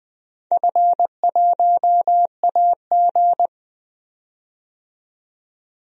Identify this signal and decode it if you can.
Morse code, heard as a keyed tone, about 20 wpm, 700 Hz. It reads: F1AG
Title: DXCC-100 Call Signs - 4 Characters - 20wpm